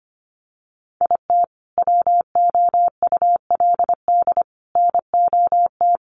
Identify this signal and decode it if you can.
Morse code, heard as a keyed tone, about 25 wpm, 700 Hz. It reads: IT WOULD NOT
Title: Sentences from Top 1500 Words 25wpm